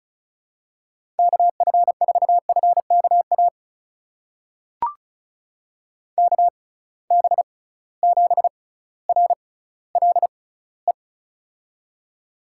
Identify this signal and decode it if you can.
Morse code, heard as a keyed tone, about 36 wpm, spaced as for 18 wpm, 700 Hz. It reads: KF4FKA KB7RLE
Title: USA Call Signs - 6 Characters - 6x Character Spacing - 35wpm